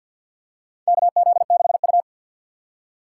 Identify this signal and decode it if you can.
Morse code, heard as a keyed tone, about 52 wpm, 700 Hz. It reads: KZ6U
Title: USA Call Signs - 3x Character Spacing - 50wpm